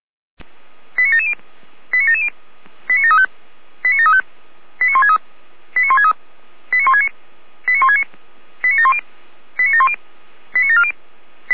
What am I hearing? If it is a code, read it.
Five-tone selective call (ZVEI), heard as quick runs of five tones, tones of about 80 ms. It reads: 89770 89770 89735 89735 89163 89163 89169 89169 89710 89710 89740